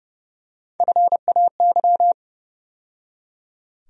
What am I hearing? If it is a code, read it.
Morse code, heard as a keyed tone, about 29 wpm, 700 Hz. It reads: FAY
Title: USA Call Sign Postfixes - 3 Letters 30wpm